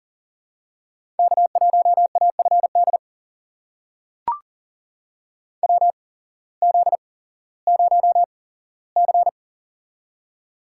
Morse code, heard as a keyed tone, about 40 wpm, spaced as for 19 wpm, 700 Hz. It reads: K1AFD WZ0C
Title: USA Call Signs - 8x Character Spacing - 40wpm